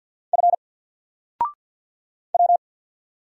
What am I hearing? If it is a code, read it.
Morse code, heard as a keyed tone, about 50 wpm, 700 Hz. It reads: F W
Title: Single Character (Letter:Number:Punctuation:Prosign) 50wpm